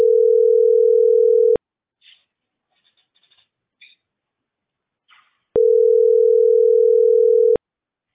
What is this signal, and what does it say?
Ringback tone (North America), 440 + 480 Hz.